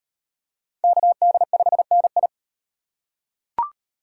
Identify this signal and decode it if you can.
Morse code, heard as a keyed tone, about 34 wpm, 700 Hz. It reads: KD5NI